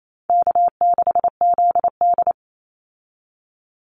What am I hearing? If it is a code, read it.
Morse code, heard as a keyed tone, about 28 wpm, 700 Hz. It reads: K6ZD
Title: Easy Call Signs - 28wpm